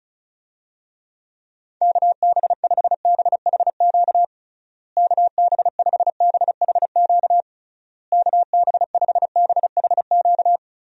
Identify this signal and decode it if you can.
Morse code, heard as a keyed tone, about 34 wpm, 700 Hz. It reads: KB5BHQ KB5BHQ KB5BHQ
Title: USA Call Signs - 6 Characters - Repeated 3x - 35wpm